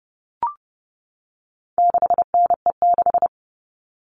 Morse code, heard as a keyed tone, about 30 wpm, 700 Hz. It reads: 6NE6
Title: All Cut Numbers - 4-Digit 30wpm